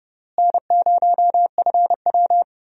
Morse code, heard as a keyed tone, about 30 wpm, 700 Hz. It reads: N0FW
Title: USA Call Signs - 4 Characters - Repeated 2x - 30wpm